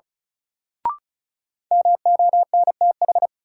Morse code, heard as a keyed tone, about 34 wpm, 700 Hz. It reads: MONTH
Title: Sentences from Top 800 Words - New Word Drill - 35wpm